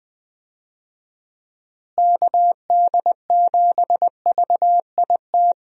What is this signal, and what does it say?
Morse code, heard as a keyed tone, about 20 wpm, 700 Hz. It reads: KD7VIT
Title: USA Call Signs - 6 Characters - Repeated 2x - 20wpm